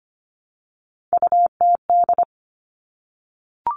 Morse code, heard as a keyed tone, about 25 wpm, 700 Hz. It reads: UTD
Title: Three Challenging Letter-Number Combinations 25wpm